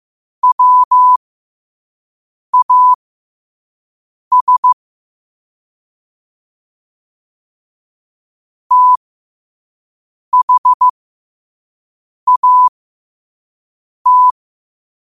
Morse code, heard as a keyed tone, about 15 wpm, spaced as for 5 wpm, 1000 Hz. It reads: WAS THAT
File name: qotdfiles / 5wpm